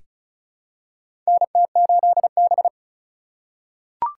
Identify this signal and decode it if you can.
Morse code, heard as a keyed tone, about 34 wpm, 700 Hz. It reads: NT8B
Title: USA Call Signs - 2x Character Spacing - 35wpm